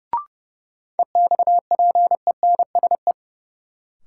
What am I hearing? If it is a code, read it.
Morse code, heard as a keyed tone, about 30 wpm, 700 Hz. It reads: EXPENSE
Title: Sets of 4 Words - New Word Drill - 30wpm